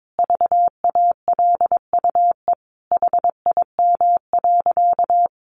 Morse code, heard as a keyed tone, about 22 wpm, 700 Hz. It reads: VALUE HIM.
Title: Sentences from Top 600 Words 22wpm